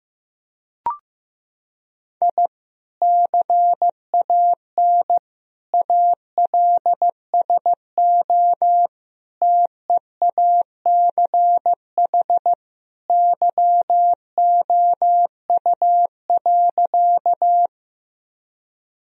Morse code, heard as a keyed tone, about 15 wpm, 700 Hz. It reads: I CAN ALSO TEACH YOU.
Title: Sentences from Top 1000 Words - Review Part 2 - 15wpm